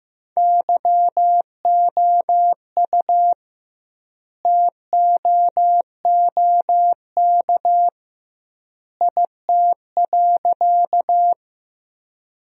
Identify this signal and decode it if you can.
Morse code, heard as a keyed tone, about 15 wpm, 700 Hz. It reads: YOU TOOK IT.
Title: 2x Word-Spacing - Sentences from Top 300 Words - 15wpm